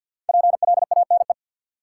Morse code, heard as a keyed tone, about 53 wpm, 700 Hz. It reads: PLANE
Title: Sentences from Top 1000 Words - Review Part 3 - New Word Drill - 50wpm